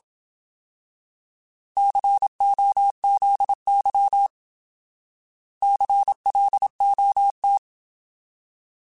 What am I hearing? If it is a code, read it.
Morse code, heard as a keyed tone, about 27 wpm, 800 Hz. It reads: COZY CLOT
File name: Alphabetical / Alphabetical mp3s / Cs @25